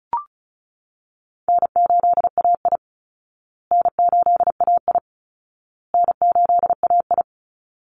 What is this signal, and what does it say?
Morse code, heard as a keyed tone, about 34 wpm, 700 Hz. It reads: N8AI N8AI N8AI
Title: Easy Call Signs Repeated 3x - 35wpm